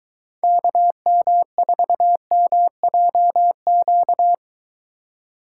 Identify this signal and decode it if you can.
Morse code, heard as a keyed tone, about 23 wpm, 700 Hz. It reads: KM4MJQ